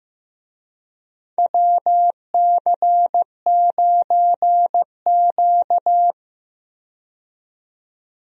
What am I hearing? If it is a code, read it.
Morse code, heard as a keyed tone, about 15 wpm, 700 Hz. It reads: WC9Q
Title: USA Call Signs - 4 Characters - 15wpm